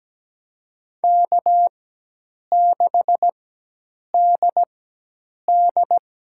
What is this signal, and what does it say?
Morse code, heard as a keyed tone, about 17 wpm, spaced as for 8 wpm, 700 Hz. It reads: K6DD
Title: USA Call Signs - 4 Characters - 4x Character Spacing - 17wpm